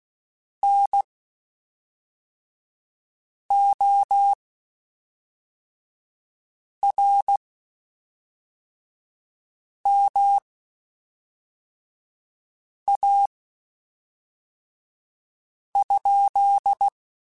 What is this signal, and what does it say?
Morse code, heard as a keyed tone, about 16 wpm, spaced as for 3 wpm, 800 Hz. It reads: NORMA?
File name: QSOs / QSOs-mp3s / WD0FBT 1@5